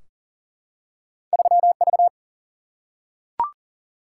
Morse code, heard as a keyed tone, about 40 wpm, 700 Hz. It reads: 3 V
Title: Single-Letter Single-Number 40wpm